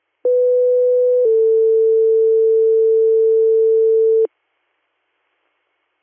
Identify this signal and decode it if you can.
Two-tone page, heard as a one-second tone, then a three-second tone, about 500 then 450 Hz. It is heard from 0.2 to 4.3 s.